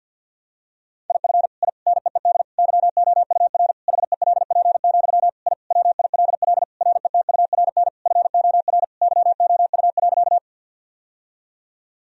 Morse code, heard as a keyed tone, about 51 wpm, 700 Hz. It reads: IF I NEED YOUR HELP, I WILL RETURN FOR YOU.
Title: Sentences from Top 1000 Words - Review Part 1 - 50wpm